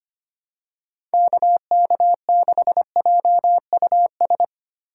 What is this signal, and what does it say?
Morse code, heard as a keyed tone, about 25 wpm, 700 Hz. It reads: KK6JUS
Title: USA Call Signs - 6 Characters - 3x Character Spacing - 25wpm